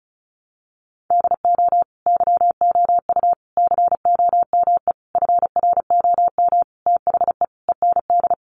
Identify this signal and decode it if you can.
Morse code, heard as a keyed tone, about 34 wpm, 700 Hz. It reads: DO YOU COME FROM THE END
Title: Sentences from Top 200 Words 35wpm